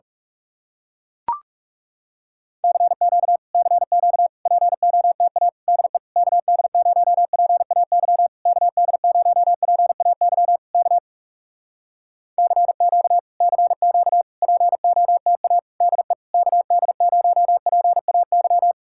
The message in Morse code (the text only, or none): CQ CQ POTA DE KD0PAY KD0PAY K CQ CQ POTA DE KD0PAY